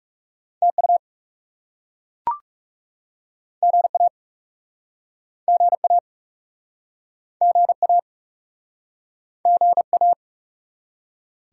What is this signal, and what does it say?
Morse code, heard as a keyed tone, about 42 wpm, 700 Hz. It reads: TU GA GA GA GA